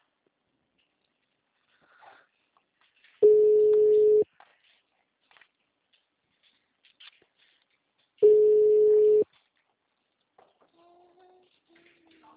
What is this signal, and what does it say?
Ringback tone (Europe, 425 Hz).